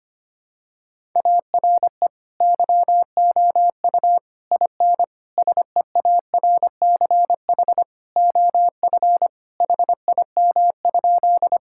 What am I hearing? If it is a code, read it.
Morse code, heard as a keyed tone, about 25 wpm, 700 Hz. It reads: ARE YOU IN SEARCH OF HIM?
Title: Sentences from Top 1000 Words - Review Part 5 - 25wpm